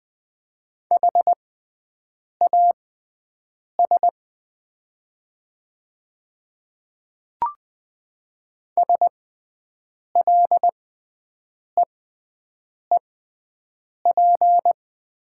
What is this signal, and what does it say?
Morse code, heard as a keyed tone, about 20 wpm, spaced as for 7 wpm, 700 Hz. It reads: HAS SLEEP
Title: 6x Character Spacing - Top 500 Words - 20wpm